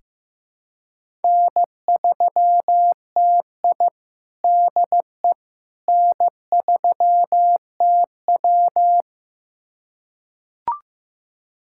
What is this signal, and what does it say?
Morse code, heard as a keyed tone, about 15 wpm, 700 Hz. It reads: N3TI DE N3TW
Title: USA Callsign DE Callsign - 4 Characters - 15wpm